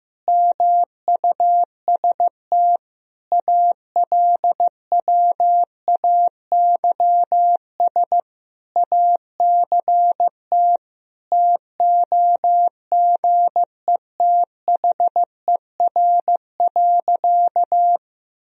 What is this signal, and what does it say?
Morse code, heard as a keyed tone, about 15 wpm, 700 Hz. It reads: MUST ALWAYS ACT TOGETHER.
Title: Sentences from Top 1000 Words - Review Part 1 - 15wpm